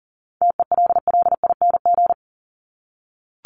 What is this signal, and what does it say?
Morse code, heard as a keyed tone, about 40 wpm, 700 Hz. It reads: TELLING
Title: ING Suffix - Top 100 Words with Base Words 40wpm